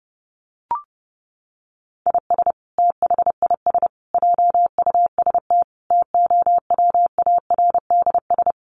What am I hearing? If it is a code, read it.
Morse code, heard as a keyed tone, about 30 wpm, 700 Hz. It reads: IS THIS JUST TOWARDS
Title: Sentences from Top 1100 Words 30wpm